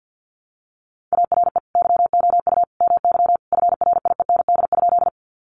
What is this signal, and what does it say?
Morse code, heard as a keyed tone, about 48 wpm, 700 Hz. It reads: ARE YOU MY FRIEND?
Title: Sentences from Top 300 Words 50wpm